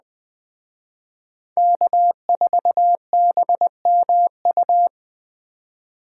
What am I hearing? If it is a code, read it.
Morse code, heard as a keyed tone, about 20 wpm, 700 Hz. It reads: K4BMU
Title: Speed Racing - USA 1x3 Call Signs 17wpm